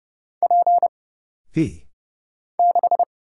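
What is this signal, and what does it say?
Morse code, heard as a keyed tone, about 31 wpm, 700 Hz. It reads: P 6
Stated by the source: Morse Code Ninja single-character practice (letters, numbers, punctuation, prosigns) at 30 wpm, by Kurt Zoglmann (AD0WE).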